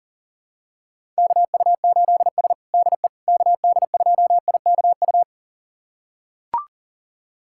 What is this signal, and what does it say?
Morse code, heard as a keyed tone, about 37 wpm, 700 Hz. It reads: KU8S DE KD2IKU
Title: USA Callsign DE Callsign 40wpm